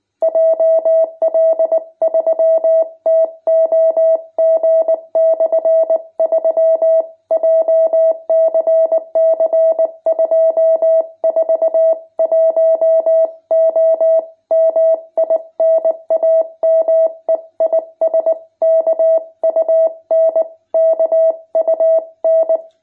Morse code, heard as a keyed tone, about 20 wpm, 650 Hz. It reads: JL3TOG/3 JCC 241O MI NAME IS KUN KUN
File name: jh8cbh / 2021cw